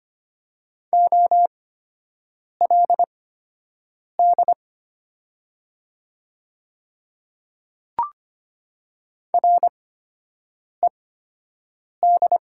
Morse code, heard as a keyed tone, about 25 wpm, spaced as for 7 wpm, 700 Hz. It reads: OLD RED
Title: 8x Character Spacing - Top 500 Words - 25wpm